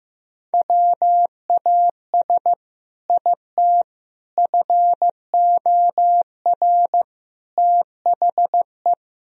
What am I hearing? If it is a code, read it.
Morse code, heard as a keyed tone, about 15 wpm, 700 Hz. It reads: WAS IT FOR THE
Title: Sentences from Top 1000 Words - Review Part 3 - 15wpm